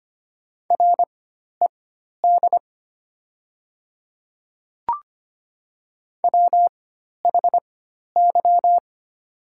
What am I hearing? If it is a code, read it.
Morse code, heard as a keyed tone, about 25 wpm, spaced as for 12 wpm, 690 Hz. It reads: RED WHY
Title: 4x Character Spacing - Top 500 Words - 25wpm